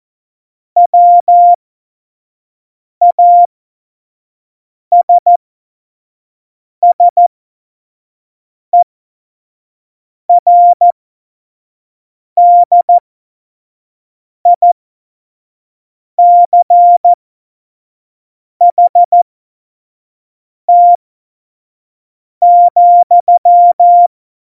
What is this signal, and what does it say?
Morse code, heard as a keyed tone, about 14 wpm, spaced as for 5 wpm, 700 Hz. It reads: WASSERDICHT,
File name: Morsekurs / Witze / Witz 12. September 17, Tempo 25